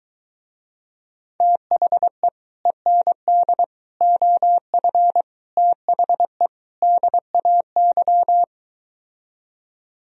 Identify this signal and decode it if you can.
Morse code, heard as a keyed tone, about 23 wpm, 700 Hz. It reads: THE END OF THE DAY